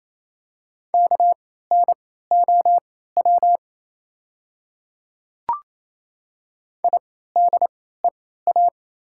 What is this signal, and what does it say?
Morse code, heard as a keyed tone, about 28 wpm, spaced as for 16 wpm, 700 Hz. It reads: KNOW IDEA
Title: 3x Character Spacing - Top 300 Words - 28wpm